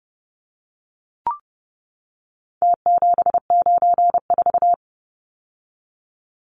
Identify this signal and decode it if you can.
Morse code, heard as a keyed tone, about 30 wpm, 700 Hz. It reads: T794
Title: Numbers - Contest Style - 4-Digit 30wpm